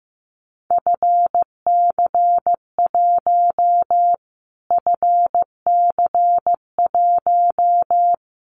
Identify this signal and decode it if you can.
Morse code, heard as a keyed tone, about 15 wpm, 700 Hz. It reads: FC1 FC1
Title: Callsign Prefixes for Top 25 DXCC - Repeated 3x - 15wpm